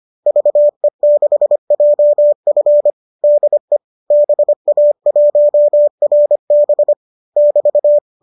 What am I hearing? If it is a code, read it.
Morse code, heard as a keyed tone, about 25 wpm, 590 Hz. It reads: VE6JF DE BA1RB <BT>
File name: qso203 25